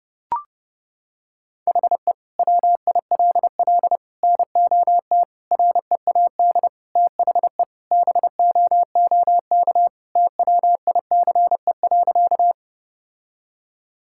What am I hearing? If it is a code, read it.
Morse code, heard as a keyed tone, about 29 wpm, 700 Hz. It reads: HE WILL NOT READ THE BOOK TWICE.
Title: Sentences from Top 1600 Words 30wpm